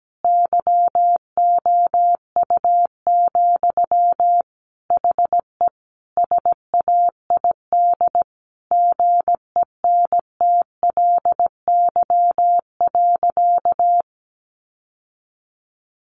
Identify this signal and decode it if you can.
Morse code, heard as a keyed tone, about 17 wpm, 700 Hz. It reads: YOU, HE SAID GENTLY.
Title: Sentences from Top 1500 Words 17wpm